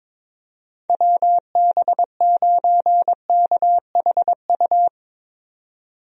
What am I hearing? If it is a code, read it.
Morse code, heard as a keyed tone, about 22 wpm, 700 Hz. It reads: WB9KHU
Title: USA Call Signs - 6 Characters - 22wpm